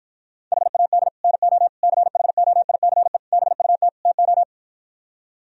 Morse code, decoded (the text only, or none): HAD NO CHOICE BUT TO